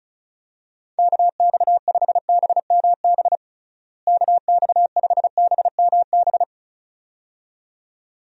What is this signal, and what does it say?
Morse code, heard as a keyed tone, about 34 wpm, 700 Hz. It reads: KX5BMB KX5BMB
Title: USA Call Signs - 6 Characters - Repeated 2x - 35wpm